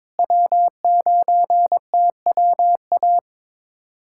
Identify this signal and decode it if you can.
Morse code, heard as a keyed tone, about 22 wpm, 700 Hz. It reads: W9TWA
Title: USA Callsign DE Callsign - 5 Characters - 22wpm